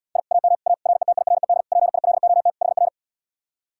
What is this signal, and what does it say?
Morse code, heard as a keyed tone, about 80 wpm, 700 Hz. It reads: I AM A RETIRED PILOT HR